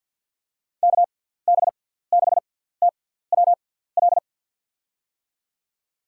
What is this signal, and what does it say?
Morse code, heard as a keyed tone, about 51 wpm, spaced as for 17 wpm, 700 Hz. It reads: KB6TWL
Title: USA Call Signs - 6 Characters - 6x Character Spacing - 50wpm